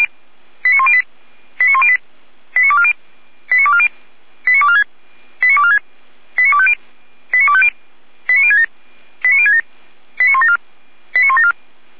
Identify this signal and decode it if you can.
Five-tone selective call (ZVEI), heard as quick runs of five tones, tones of about 70 ms. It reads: 89108 89108 89240 89240 89246 89246 89250 89067 89067 89174 89174